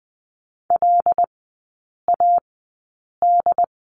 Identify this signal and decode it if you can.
Morse code, heard as a keyed tone, about 20 wpm, spaced as for 8 wpm, 700 Hz. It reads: LAD
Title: 074 Lesson 13e - Letter L - Three Characters 20wpm